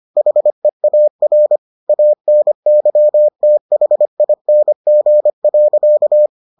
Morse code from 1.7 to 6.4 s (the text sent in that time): ANYTHING.